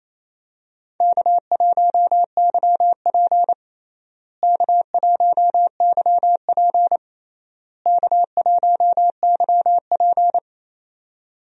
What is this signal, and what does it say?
Morse code, heard as a keyed tone, about 28 wpm, 700 Hz. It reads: K1YP K1YP K1YP
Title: USA Call Signs - 4 Characters - Repeated 3x - 28wpm